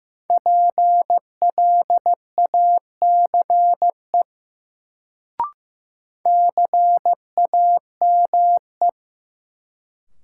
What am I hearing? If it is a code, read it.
Morse code, heard as a keyed tone, about 15 wpm, 700 Hz. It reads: PLACE CAME
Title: Sentences from Top 200 Words - New Word Drill - 15wpm